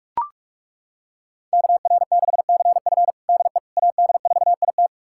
Morse code, heard as a keyed tone, about 41 wpm, 700 Hz. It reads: KR6CF DE AD4IT
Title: USA Callsign DE Callsign - 5 Characters - 45wpm